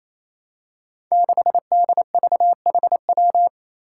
Morse code, heard as a keyed tone, about 28 wpm, 700 Hz. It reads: 6DVHW
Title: Five Challenging Letter-Number Combinations 28wpm